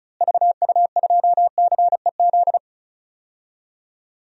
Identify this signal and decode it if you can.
Morse code, heard as a keyed tone, about 36 wpm, 700 Hz. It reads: VU2CEZ
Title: DXCC-100 Call Signs - 6 Characters - Repeated 2x - 35wpm